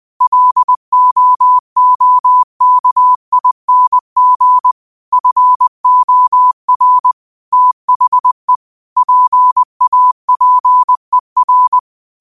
Morse code, heard as a keyed tone, about 20 wpm, 1000 Hz. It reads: LOOKING FOR THE PAPER
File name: qotdfiles / 20wpm